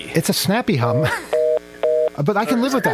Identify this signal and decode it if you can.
Reorder tone (North America), 480 + 620 Hz, with speech.
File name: mp3files / off the hook overtime -128